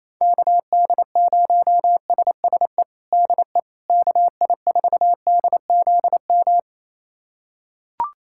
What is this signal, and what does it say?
Morse code, heard as a keyed tone, about 28 wpm, 700 Hz. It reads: KD0SSE DE KI4DZM
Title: USA Callsign DE Callsign 28wpm